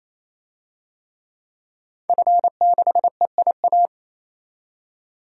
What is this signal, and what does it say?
Morse code, heard as a keyed tone, about 27 wpm, 700 Hz. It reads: F6EIA
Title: DXCC-100 Call Signs - 5 Characters - 28wpm